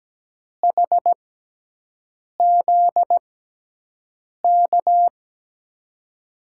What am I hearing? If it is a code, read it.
Morse code, heard as a keyed tone, about 17 wpm, spaced as for 6 wpm, 700 Hz. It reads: HZK
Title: USA Call Signs - 5 Characters - 6x Character Spacing - 17wpm